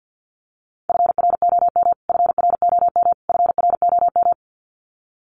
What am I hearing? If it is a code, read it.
Morse code, heard as a keyed tone, about 50 wpm, 700 Hz. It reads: FROM FROM FROM
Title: Top 500 Words Repeated 50wpm